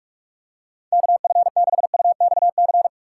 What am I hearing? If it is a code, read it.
Morse code, heard as a keyed tone, about 47 wpm, 700 Hz. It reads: KF6UXC